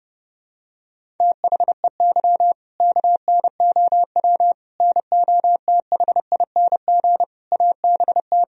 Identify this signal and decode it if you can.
Morse code, heard as a keyed tone, about 32 wpm, 700 Hz. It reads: THEY KNOW NOTHING ABT
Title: Sentences from Top 500 Words and Abbreviations 30wpm